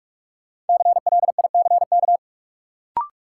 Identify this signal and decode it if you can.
Morse code, heard as a keyed tone, about 47 wpm, 700 Hz. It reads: CLICK
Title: Sentences from Top 1100 Words - New Word Drill - 45wpm